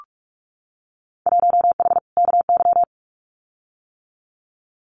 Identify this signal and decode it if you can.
Morse code, heard as a keyed tone, about 44 wpm, 700 Hz. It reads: 1H KY
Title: Winter Field Day Exchanges 45wpm